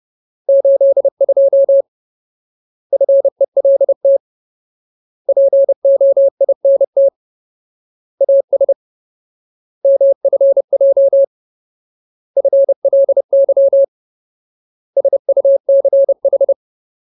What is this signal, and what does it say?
Morse code, heard as a keyed tone, about 30 wpm, 550 Hz. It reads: 82 FELT POINT AS MFJ FLY SUCH
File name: Ultimate Word List / 30WPM Ultimate